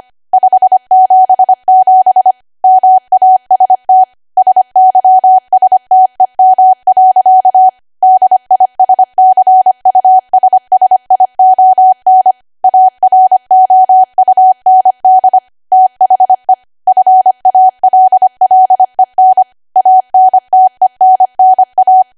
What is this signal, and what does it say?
Morse code, heard as a keyed tone, about 25 wpm, 750 Hz. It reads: 577 MAST SYSTEM. DISCUSSION AROUND THE FALLEN ANTENNA
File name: Archive / 25 WPM / 25WPM